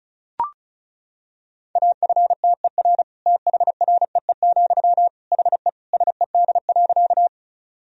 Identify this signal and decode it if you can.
Morse code, heard as a keyed tone, about 34 wpm, 690 Hz. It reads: AFTER THREE, HE SED.
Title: Sentences from Top 200 Words and Abbreviations 35wpm